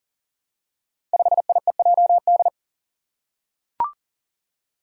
Morse code, heard as a keyed tone, about 40 wpm, 700 Hz. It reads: 5IEJD
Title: Five Challenging Letter-Number Combinations 40wpm